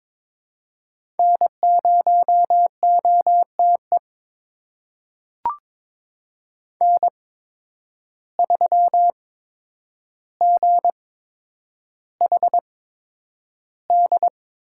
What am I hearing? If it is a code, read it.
Morse code, heard as a keyed tone, about 22 wpm, spaced as for 9 wpm, 700 Hz. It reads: N0OTE N3GHD
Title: USA Call Signs - 5 Characters - 8x Character Spacing - 22wpm